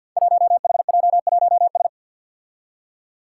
Morse code, heard as a keyed tone, about 46 wpm, 700 Hz. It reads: 1HJ1S
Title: Five Challenging Letter-Number Combinations 50wpm